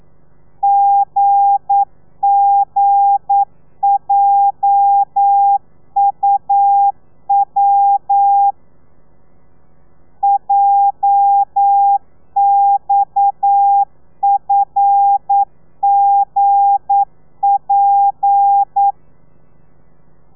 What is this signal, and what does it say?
Morse code, heard as a keyed tone, about 9 wpm, 800 Hz. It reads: GGJUW JXFGP
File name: telegrafia / i2tfj / Lez8 40